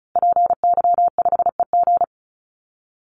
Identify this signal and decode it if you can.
Morse code, heard as a keyed tone, about 34 wpm, 700 Hz. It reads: PY5EG
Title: Call Signs Repeated 2x - 35wpm